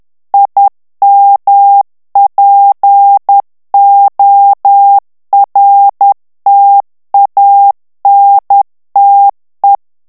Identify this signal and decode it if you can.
Morse code, heard as a keyed tone, about 10 wpm, 800 Hz. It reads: IMPORTANTE
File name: ITALIEN texte clair 2 vitesse 10 mots minute